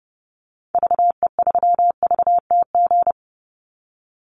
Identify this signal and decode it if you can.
Morse code, heard as a keyed tone, about 30 wpm, 700 Hz. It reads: VE3VTG
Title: Call Signs - 30wpm